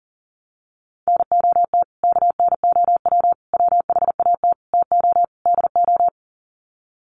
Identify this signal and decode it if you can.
Morse code, heard as a keyed tone, about 40 wpm, 700 Hz. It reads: NOT KNOW WHAT TO DO
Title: Sets of 5 Words 40wpm